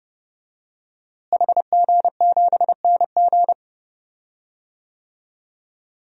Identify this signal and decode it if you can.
Morse code, heard as a keyed tone, about 31 wpm, 700 Hz. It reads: HG7NG
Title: DXCC-100 Call Signs - 5 Characters - Repeated 2x - 30wpm